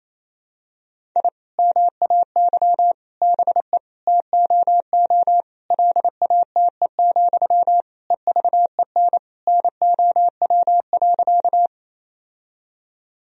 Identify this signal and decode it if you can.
Morse code, heard as a keyed tone, about 28 wpm, 700 Hz. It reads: I MAY BE TOO LATE, EVEN NOW.
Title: Sentences from Top 1000 Words - Review Part 1 - 28wpm